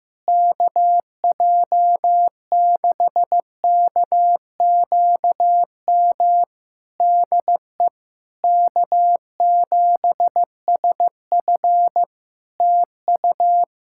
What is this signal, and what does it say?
Morse code, heard as a keyed tone, about 15 wpm, 700 Hz. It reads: KJ6KQM DE K7SF TU
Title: Practice QSO - Exchange 4 - Ending - 15wpm